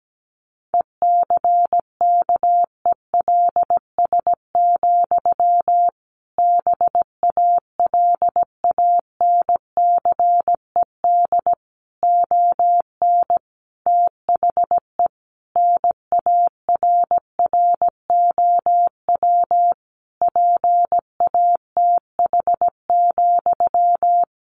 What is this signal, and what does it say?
Morse code, heard as a keyed tone, about 17 wpm, 700 Hz. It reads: ECKELS, BALANCED ON THE NARROW PATH,
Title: A Sound Of Thunder 17wpm